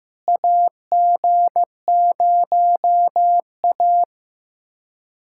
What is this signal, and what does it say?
Morse code, heard as a keyed tone, about 16 wpm, 700 Hz. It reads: AG0A
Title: USA Callsign DE Callsign - 4 Characters - 15wpm